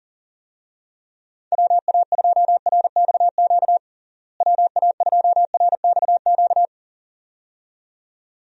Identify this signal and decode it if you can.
Morse code, heard as a keyed tone, about 40 wpm, 700 Hz. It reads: WA2RXQ WA2RXQ
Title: USA Call Signs - 6 Characters - Repeated 2x - 40wpm